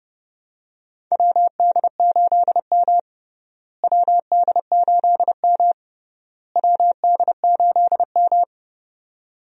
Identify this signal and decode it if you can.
Morse code, heard as a keyed tone, about 32 wpm, 700 Hz. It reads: WD8M WD8M WD8M
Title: USA Call Signs - 4 Characters - Repeated 3x - 30wpm